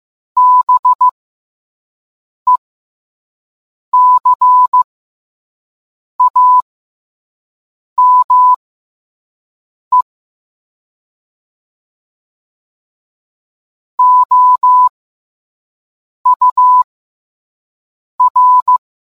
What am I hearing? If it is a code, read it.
Morse code, heard as a keyed tone, about 15 wpm, spaced as for 5 wpm, 1000 Hz. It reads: BECAME OUR